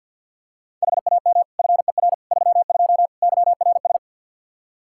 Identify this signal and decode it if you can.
Morse code, heard as a keyed tone, about 46 wpm, 700 Hz. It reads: HAM FER 32 YRS
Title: Practice QSO - Exchange 3 - More Info - 50wpm